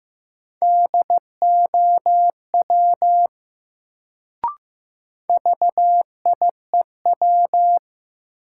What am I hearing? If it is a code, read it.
Morse code, heard as a keyed tone, about 15 wpm, 700 Hz. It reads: DOW VIEW
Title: Sets of 5 Words - New Word Drill - 15wpm